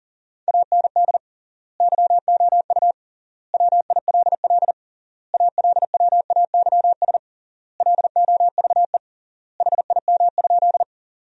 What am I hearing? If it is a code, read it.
Morse code, heard as a keyed tone, about 40 wpm, 700 Hz. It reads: AND YOU WILL ALWAYS LOVE HIM?
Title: 3x Word-Spacing - Sentences from Top 400 Words - 40wpm